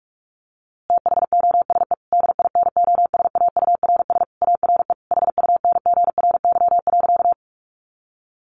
Morse code, heard as a keyed tone, about 44 wpm, 700 Hz. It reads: THOSE DINOSAURS ARE HUNGRY.
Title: A Sound Of Thunder 45wpm